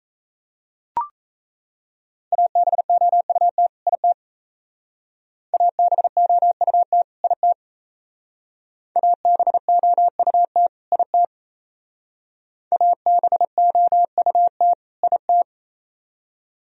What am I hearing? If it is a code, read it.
Morse code, heard as a keyed tone, about 34 wpm, 700 Hz. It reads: ABOUT IT ABOUT IT ABOUT IT ABOUT IT
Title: Speed-Racing - Sets of 2 Words 28wpm